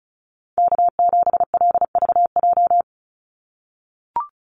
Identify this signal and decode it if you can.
Morse code, heard as a keyed tone, about 36 wpm, 700 Hz. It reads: K7LVJ
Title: Call Signs - 35wpm